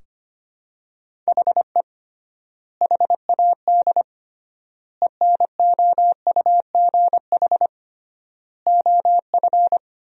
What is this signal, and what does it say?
Morse code, heard as a keyed tone, about 25 wpm, 700 Hz. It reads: HE HAD ENOUGH OF
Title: 3x Word-Spacing - Sentences from Top 400 Words - 25wpm